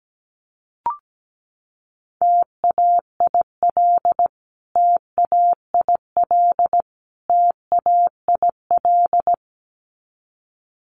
Morse code, heard as a keyed tone, about 17 wpm, 700 Hz. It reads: TAIL TAIL TAIL
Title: Top 500 Words Repeated 17wpm